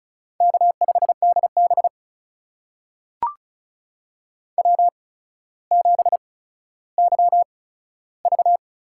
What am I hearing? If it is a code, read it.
Morse code, heard as a keyed tone, about 36 wpm, spaced as for 15 wpm, 700 Hz. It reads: K5DB W7YV
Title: USA Call Signs - 8x Character Spacing - 35wpm